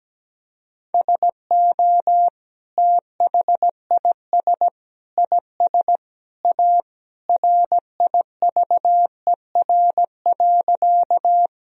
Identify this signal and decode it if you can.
Morse code, heard as a keyed tone, about 17 wpm, 700 Hz. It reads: SO THIS IS A RIVER.
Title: Sentences from Top 1000 Words - Review Part 1 - 17wpm